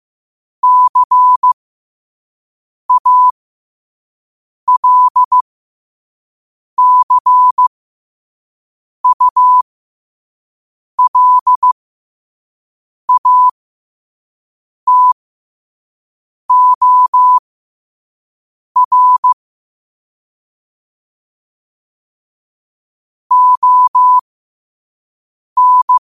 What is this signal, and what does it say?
Morse code, heard as a keyed tone, about 15 wpm, spaced as for 5 wpm, 1000 Hz. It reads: CALCULATOR ON
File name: qotdfiles / 5wpm